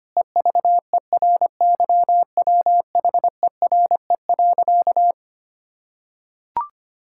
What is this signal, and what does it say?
Morse code, heard as a keyed tone, about 25 wpm, 700 Hz. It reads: EVERYWHERE.
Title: Sentences from Top 1600 Words 25wpm